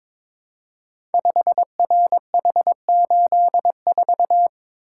Morse code, heard as a keyed tone, about 22 wpm, 700 Hz. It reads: 5RH84
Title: Code Groups - 5 Letter:Numbers - 22wpm